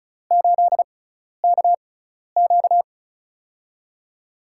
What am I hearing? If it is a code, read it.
Morse code, heard as a keyed tone, about 36 wpm, spaced as for 12 wpm, 700 Hz. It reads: 8KQ
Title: USA Call Signs - 5 Characters - 6x Character Spacing - 35wpm